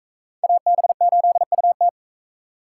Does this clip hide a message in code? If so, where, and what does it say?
Yes: Morse code from 0.4 to 1.9 s, heard as a keyed tone, about 41 wpm, 700 Hz. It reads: AB8UT